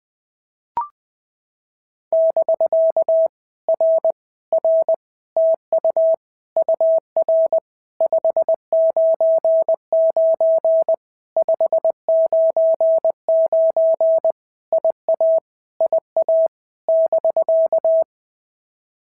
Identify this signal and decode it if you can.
Morse code, heard as a keyed tone, about 20 wpm, 650 Hz. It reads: <BK> R R TU UR 599 599 IA IA <BK>
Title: POTA QSO Practice - 20wpm